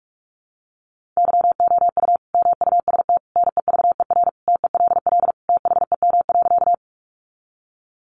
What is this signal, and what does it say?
Morse code, heard as a keyed tone, about 44 wpm, 700 Hz. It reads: YOU MUST NEVER TELL THEM.
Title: Sentences from Top 900 Words 45wpm